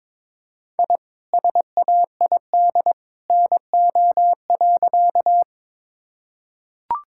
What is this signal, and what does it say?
Morse code, heard as a keyed tone, about 22 wpm, 700 Hz. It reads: I SAID NO.
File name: practice / 22wpm / Sentences with 2-Word Phrases - 22wpm